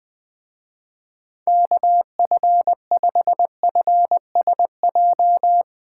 Morse code, heard as a keyed tone, about 20 wpm, 700 Hz. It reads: KF5FSJ